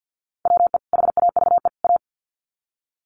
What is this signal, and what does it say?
Morse code, heard as a keyed tone, about 50 wpm, 700 Hz. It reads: WE HAVE A
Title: Sets of 3 Words 50wpm